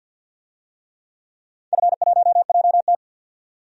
Morse code, heard as a keyed tone, about 50 wpm, 700 Hz. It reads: F1JT
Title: DXCC-100 Call Signs - 4 Characters - 50wpm